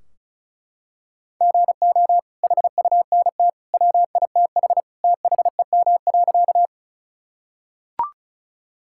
Morse code, heard as a keyed tone, about 36 wpm, 700 Hz. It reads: GO HUNT WITH THEM.
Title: Sentences from Top 1600 Words 35wpm